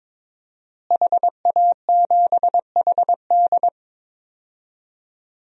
Morse code, heard as a keyed tone, about 22 wpm, 700 Hz. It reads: HA7HD
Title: DXCC-100 Call Signs - 5 Characters - Repeated 3x - 22wpm